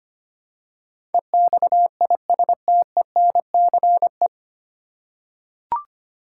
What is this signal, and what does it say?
Morse code, heard as a keyed tone, about 25 wpm, 690 Hz. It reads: EXISTENCE
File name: practice / 25wpm / 9 Letter Words - First 3 Letters Given - 25wpm